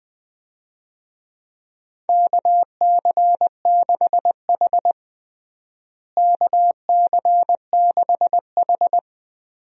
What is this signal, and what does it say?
Morse code, heard as a keyed tone, about 21 wpm, 700 Hz. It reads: KC6H KC6H
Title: USA Call Signs - 4 Characters - Repeated 2x - 20wpm